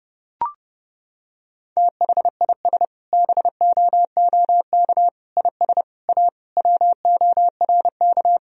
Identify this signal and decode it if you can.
Morse code, heard as a keyed tone, about 32 wpm, 700 Hz. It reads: THIS BOOK IS A WORK
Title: Sentences from Top 1000 Words - Review Part 3 - 30wpm